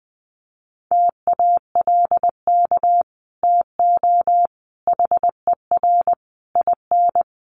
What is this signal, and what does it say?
Morse code, heard as a keyed tone, about 20 wpm, 700 Hz. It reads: TALK TO HER IN
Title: Sentences from Top 1500 Words 20wpm